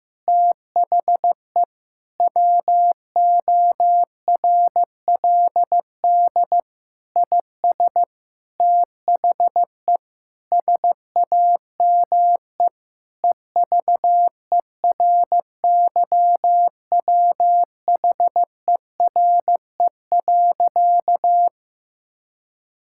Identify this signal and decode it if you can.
Morse code, heard as a keyed tone, about 15 wpm, 700 Hz. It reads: THE WORLD IS THE SAME EVERYWHERE.
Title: Sentences from Top 1600 Words 15wpm